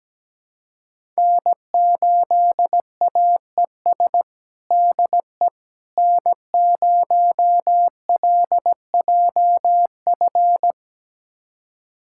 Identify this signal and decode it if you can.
Morse code, heard as a keyed tone, about 17 wpm, 700 Hz. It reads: N8AES DE N0LJF
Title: USA Callsign DE Callsign - 5 Characters - 17wpm